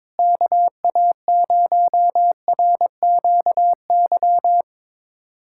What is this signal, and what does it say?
Morse code, heard as a keyed tone, about 22 wpm, 700 Hz. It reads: KA0RQY
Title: USA Callsign DE Callsign - 6 Characters - 22wpm